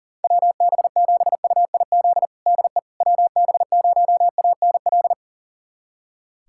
Morse code, heard as a keyed tone, about 40 wpm, 700 Hz. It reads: WB7UIZ DE WB0ANL
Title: USA Callsign DE Callsign - 6 Characters - 40wpm